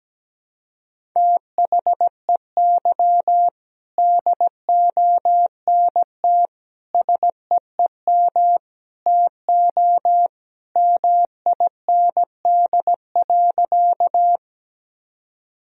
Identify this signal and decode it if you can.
Morse code, heard as a keyed tone, about 17 wpm, 700 Hz. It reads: THEY DONT SEEM TO MIND.
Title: Sentences from Top 1000 Words - Review Part 1 - 17wpm